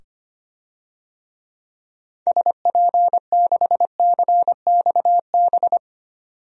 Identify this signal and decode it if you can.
Morse code, heard as a keyed tone, about 25 wpm, 690 Hz. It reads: SP6CXB
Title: DXCC-100 Call Signs - 6 Characters - 25wpm